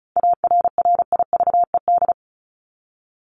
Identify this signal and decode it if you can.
Morse code, heard as a keyed tone, about 34 wpm, 700 Hz. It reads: ARRIVED
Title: ED Suffix - Top 100 Words with Base Words 35wpm